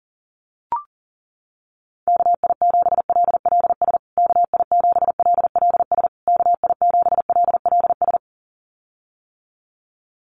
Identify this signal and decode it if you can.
Morse code, heard as a keyed tone, about 40 wpm, 700 Hz. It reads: KI7LLS KI7LLS KI7LLS
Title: USA Call Signs Repeated 3x 40wpm